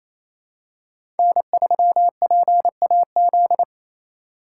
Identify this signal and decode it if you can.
Morse code, heard as a keyed tone, about 28 wpm, 700 Hz. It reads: N3PAZ
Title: USA Call Signs - 5 Characters - 2x Character Spacing - 28wpm